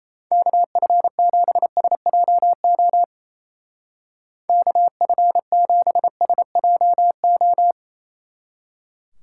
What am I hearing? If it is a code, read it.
Morse code, heard as a keyed tone, about 34 wpm, 700 Hz. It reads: KF7SJO KF7SJO
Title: Speed Racing - USA 2x3 Call Signs 28wpm